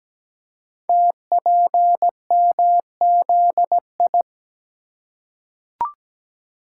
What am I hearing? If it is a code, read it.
Morse code, heard as a keyed tone, about 17 wpm, 700 Hz. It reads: TPMZI
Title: Code Groups - 5 Letters - 17wpm